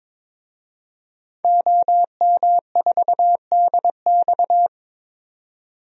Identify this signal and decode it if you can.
Morse code, heard as a keyed tone, about 22 wpm, 700 Hz. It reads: OM4DX
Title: DXCC-100 Call Signs - 5 Characters - 22wpm